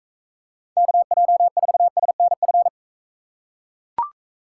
Morse code, heard as a keyed tone, about 42 wpm, 700 Hz. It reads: KJ4SNF